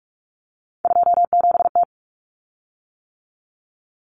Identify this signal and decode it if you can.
Morse code, heard as a keyed tone, about 44 wpm, 700 Hz. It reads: 27T
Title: Common Cut Numbers - 3-Digit 45wpm